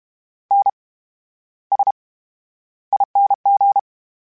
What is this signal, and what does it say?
Morse code, heard as a keyed tone, about 32 wpm, spaced as for 13 wpm, 800 Hz. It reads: NSING
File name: Special INGs / Special INGs-mp3s / VerySlowToFaster Ing @30